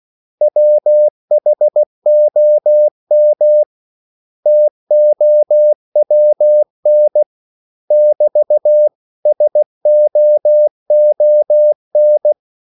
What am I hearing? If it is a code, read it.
Morse code, heard as a keyed tone, about 16 wpm, 600 Hz. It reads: WHOM TOWN <BT> SOON
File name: CW / Practice / 500MostCommonEnglishWords16